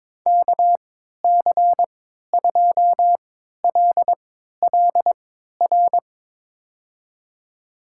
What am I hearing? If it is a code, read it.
Morse code, heard as a keyed tone, about 22 wpm, spaced as for 12 wpm, 700 Hz. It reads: KC2LLR
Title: USA Call Signs - 6 Characters - 3x Character Spacing - 22wpm